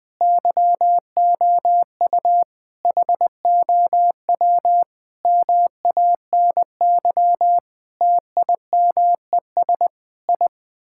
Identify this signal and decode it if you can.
Morse code, heard as a keyed tone, about 20 wpm, 700 Hz. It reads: YOU HOW MANY TIMES I
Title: Sets of 8 Words - 20wpm